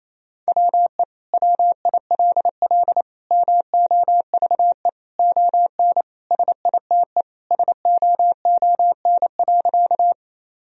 Morse code, heard as a keyed tone, about 28 wpm, 700 Hz. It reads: WE WILL MOVE ON SITE SOON.
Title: Sentences from Top 1000 Words - Review Part 4 - 28wpm